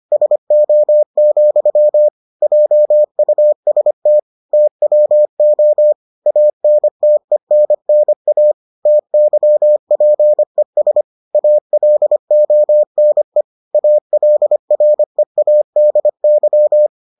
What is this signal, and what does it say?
Morse code, heard as a keyed tone, about 25 wpm, 600 Hz. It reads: SO, JUST TWO ANTENNA TYPES ALONE ALREADY